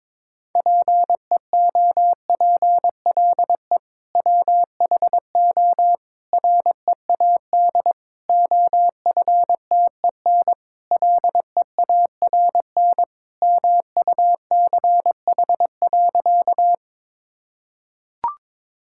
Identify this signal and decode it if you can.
Morse code, heard as a keyed tone, about 22 wpm, 700 Hz. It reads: PEOPLE WHO READ OFTEN LEARN MUCH.
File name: practice / 22wpm / Sentences with 2-Word Phrases - 22wpm